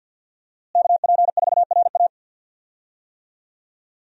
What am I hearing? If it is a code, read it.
Morse code, heard as a keyed tone, about 52 wpm, 700 Hz. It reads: KP4RA
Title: DXCC-100 Call Signs - 5 Characters - Repeated 3x - 50wpm